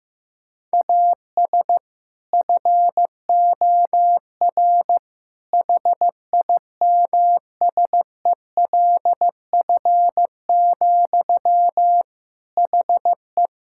Morse code, heard as a keyed tone, about 15 wpm, 700 Hz. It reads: AS FOR HIMSELF, HE
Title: Sentences from Top 1600 Words 15wpm